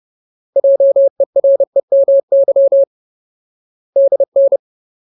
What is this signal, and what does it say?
Morse code, heard as a keyed tone, about 30 wpm, 550 Hz. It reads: JEREMY DN
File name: Ultimate Word List / 30WPM Ultimate